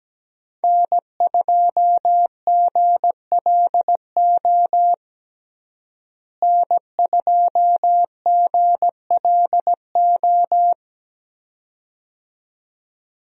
Morse code, heard as a keyed tone, about 17 wpm, 700 Hz. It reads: N2GLO N2GLO
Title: USA Call Signs - 5 Characters - Repeated 2x - 17wpm